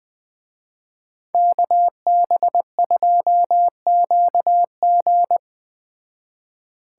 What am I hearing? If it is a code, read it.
Morse code, heard as a keyed tone, about 21 wpm, 700 Hz. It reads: KB2QG